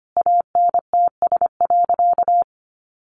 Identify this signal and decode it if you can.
Morse code, heard as a keyed tone, about 25 wpm, 700 Hz. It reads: ANTS.
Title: Easy Phrases 25wpm